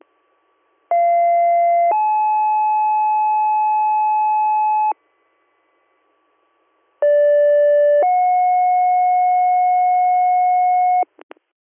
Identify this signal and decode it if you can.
Two-tone page, heard as a one-second tone, then a three-second tone, about 690 then 860 Hz, about 580 then 730 Hz. It is heard from 0.9 to 11.0 s.